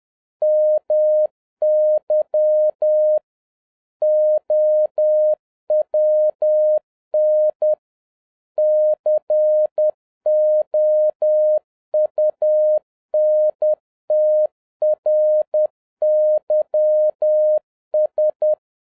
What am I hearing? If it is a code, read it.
Morse code, heard as a keyed tone, about 10 wpm, 610 Hz. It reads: MY OWN COUNTRYS